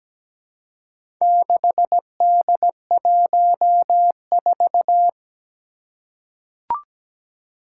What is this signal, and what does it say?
Morse code, heard as a keyed tone, about 17 wpm, 700 Hz. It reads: 6D14
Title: Four Challenging Letter-Number Combinations 17wpm